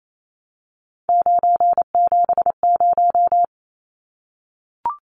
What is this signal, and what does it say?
Morse code, heard as a keyed tone, about 28 wpm, 700 Hz. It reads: 970
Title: Numbers - 3-Digit 28wpm